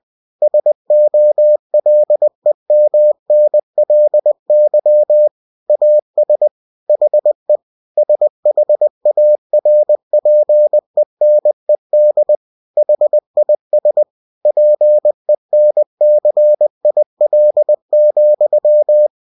Morse code, heard as a keyed tone, about 20 wpm, 600 Hz. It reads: SOLEMNLY AS HE SHARPENED HIS PENCIL,
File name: cw-books / winnie-the-pooh / winnie-20wpm-page-0063